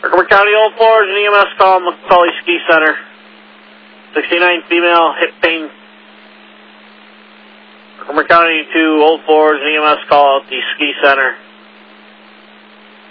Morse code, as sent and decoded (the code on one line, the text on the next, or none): none